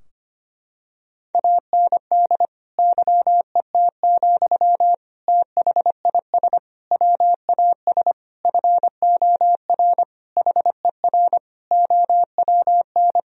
Morse code, heard as a keyed tone, about 25 wpm, 700 Hz. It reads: AND YET, THIS WAS FOR HER OWN
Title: Sentences from Top 1000 Words - Review Part 3 - 25wpm